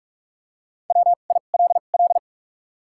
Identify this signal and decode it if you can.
Morse code, heard as a keyed tone, about 44 wpm, 700 Hz. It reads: W I L L
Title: 2x Character Spacing - Top 300 Words - 45wpm